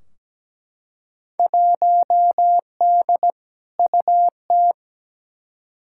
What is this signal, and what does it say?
Morse code, heard as a keyed tone, about 17 wpm, 700 Hz. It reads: 1D UT
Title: Field Day Exchanges - 17wpm